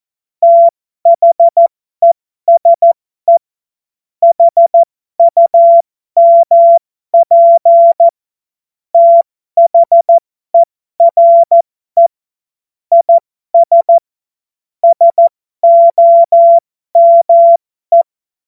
Morse code, heard as a keyed tone, about 14 wpm, spaced as for 12 wpm, 680 Hz. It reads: THESE HUMP THERE IS SOME
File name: Morsekurs / Witze